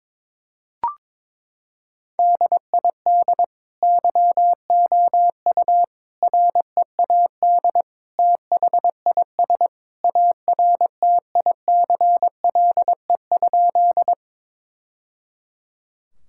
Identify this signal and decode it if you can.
Morse code, heard as a keyed tone, about 22 wpm, 700 Hz. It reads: DID YOU READ THIS ARTICLE?
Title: Sentences from Top 1000 Words - Review Part 5 - 22wpm